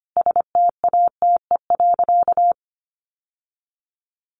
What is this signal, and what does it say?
Morse code, heard as a keyed tone, about 25 wpm, 700 Hz. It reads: STATE.
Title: Sentences from Top 1300 Words 25wpm